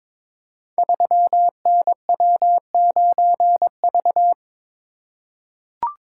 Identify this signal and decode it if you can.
Morse code, heard as a keyed tone, about 22 wpm, 700 Hz. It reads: 3NW9V
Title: Code Groups - 5 Letter:Numbers - 22wpm